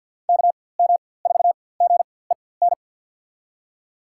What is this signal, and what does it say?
Morse code, heard as a keyed tone, about 47 wpm, spaced as for 23 wpm, 700 Hz. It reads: KM4GEN
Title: USA Call Signs - 6 Characters - 4x Character Spacing - 50wpm